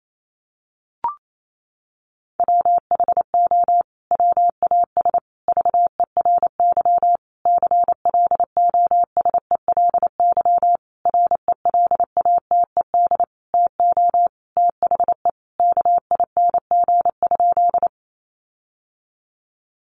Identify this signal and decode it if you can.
Morse code, heard as a keyed tone, about 28 wpm, 700 Hz. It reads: WHO WAS VERY CLOSELY RELATED TO THE KING?
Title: Sentences from Top 1400 Words 28wpm